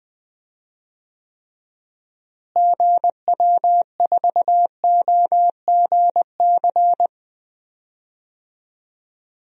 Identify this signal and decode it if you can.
Morse code, heard as a keyed tone, about 21 wpm, 700 Hz. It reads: GW4OGC
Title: DXCC-100 Call Signs - 6 Characters - Repeated 3x - 20wpm